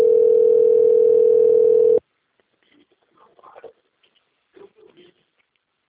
Ringback tone (North America), 440 + 480 Hz.